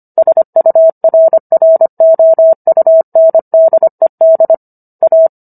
Morse code, heard as a keyed tone, about 25 wpm, 650 Hz. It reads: SURROUNDED A